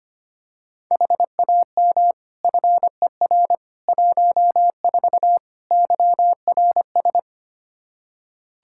Morse code, heard as a keyed tone, about 25 wpm, 700 Hz. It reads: HAM FER 14 YRS
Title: Practice QSO - Exchange 3 - More Info - 25wpm